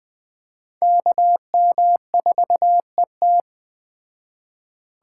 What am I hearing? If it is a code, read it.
Morse code, heard as a keyed tone, about 21 wpm, 700 Hz. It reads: KM4ET